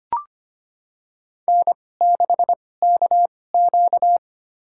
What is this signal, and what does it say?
Morse code, heard as a keyed tone, about 25 wpm, spaced as for 18 wpm, 700 Hz. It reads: N6KQ
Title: USA Call Signs - 2x Character Spacing - 25wpm